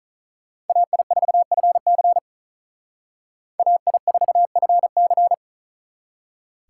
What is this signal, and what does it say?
Morse code, heard as a keyed tone, about 42 wpm, 700 Hz. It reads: AI4FC AI4FC
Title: Speed Racing - USA 2x2 Call Signs 35wpm